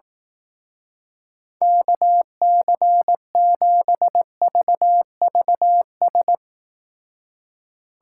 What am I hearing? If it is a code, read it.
Morse code, heard as a keyed tone, about 19 wpm, 700 Hz. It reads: KC7VVS